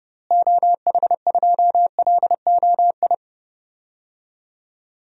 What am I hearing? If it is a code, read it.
Morse code, heard as a keyed tone, about 30 wpm, 700 Hz. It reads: OH2LOI
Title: DXCC-100 Call Signs - 6 Characters - Repeated 3x - 30wpm